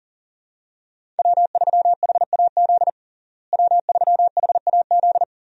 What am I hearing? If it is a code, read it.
Morse code, heard as a keyed tone, about 40 wpm, 700 Hz. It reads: W3HAZ W3HAZ
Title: USA Call Signs - 5 Characters - Repeated 3x - 40wpm